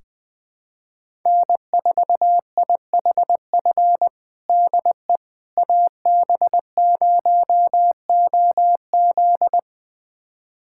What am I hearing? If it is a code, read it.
Morse code, heard as a keyed tone, about 21 wpm, 700 Hz. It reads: N4IHF DE AB0OZ
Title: USA Callsign DE Callsign - 5 Characters - 20wpm